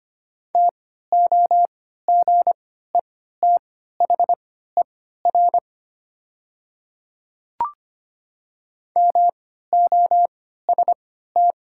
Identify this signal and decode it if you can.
Morse code, heard as a keyed tone, about 25 wpm, spaced as for 14 wpm, 700 Hz. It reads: TOGETHER MOST
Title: 3x Character Spacing - Top 300 Words - 25wpm